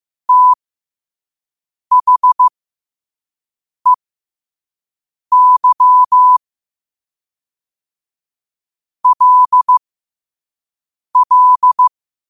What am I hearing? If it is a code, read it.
Morse code, heard as a keyed tone, about 15 wpm, spaced as for 5 wpm, 1000 Hz. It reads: THEY LL
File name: qotdfiles / 5wpm